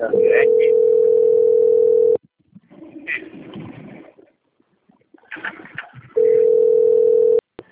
Ringback tone (North America), 440 + 480 Hz.